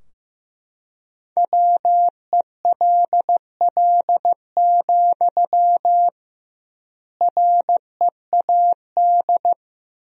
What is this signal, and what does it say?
Morse code, heard as a keyed tone, about 15 wpm, 700 Hz. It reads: WELL, READ
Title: 2x Word-Spacing - Sentences from Top 200 Words - 15wpm